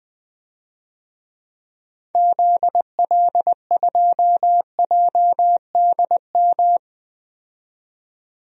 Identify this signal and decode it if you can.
Morse code, heard as a keyed tone, about 20 wpm, 700 Hz. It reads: ZL2JDM
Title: DXCC-100 Call Signs - 6 Characters - 20wpm